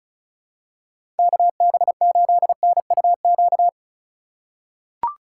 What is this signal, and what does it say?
Morse code, heard as a keyed tone, about 34 wpm, 690 Hz. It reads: KB8NUQ
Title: USA Call Signs - 6 Characters - 2x Character Spacing - 35wpm